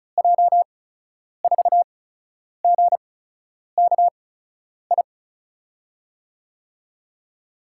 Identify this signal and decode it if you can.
Morse code, heard as a keyed tone, about 36 wpm, spaced as for 10 wpm, 700 Hz. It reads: J4GKI
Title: USA Call Signs - 6 Characters - 8x Character Spacing - 35wpm